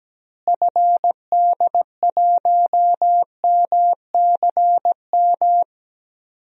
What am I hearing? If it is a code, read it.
Morse code, heard as a keyed tone, about 17 wpm, 700 Hz. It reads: FD1MCM
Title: DXCC-100 Call Signs - 6 Characters - Repeated 3x - 17wpm